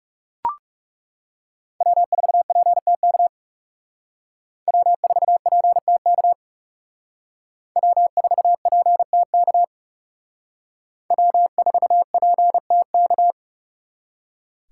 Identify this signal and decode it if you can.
Morse code, heard as a keyed tone, about 38 wpm, 700 Hz. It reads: W4PTK W4PTK W4PTK W4PTK